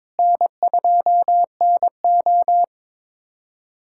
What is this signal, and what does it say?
Morse code, heard as a keyed tone, about 22 wpm, 700 Hz. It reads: N2NO
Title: USA Callsign DE Callsign 22wpm